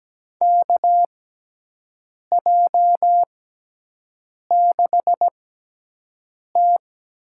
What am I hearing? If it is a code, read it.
Morse code, heard as a keyed tone, about 17 wpm, spaced as for 6 wpm, 700 Hz. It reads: KJ6T
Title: USA Call Signs - 6 Characters - 6x Character Spacing - 17wpm